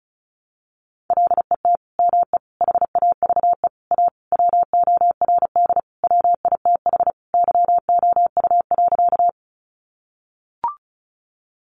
Morse code, heard as a keyed tone, about 34 wpm, 700 Hz. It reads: LET ME HAVE A WORD WITH YOU.
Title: Sentences from Top 300 Words 35wpm